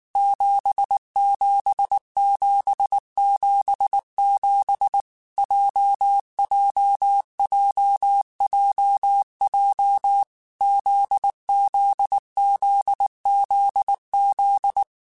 Morse code, heard as a keyed tone, about 19 wpm, 800 Hz. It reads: 77777 JJJJJ ZZZZZ
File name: Basic Sets / Basic Letter Groups Collective7 / JZ17@18